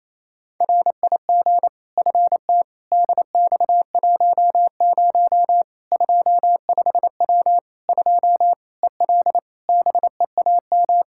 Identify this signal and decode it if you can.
Morse code, heard as a keyed tone, about 28 wpm, 700 Hz. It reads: RIG FT DX10 25W 2 EL BEAM
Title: Practice QSO - Exchange 2 - Info - 28wpm